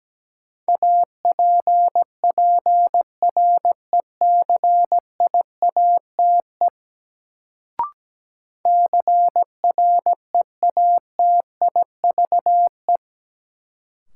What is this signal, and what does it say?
Morse code, heard as a keyed tone, about 17 wpm, 700 Hz. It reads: APPRECIATE CREATIVE
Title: Sentences from Top 1300 Words - New Word Drill Alt - 17wpm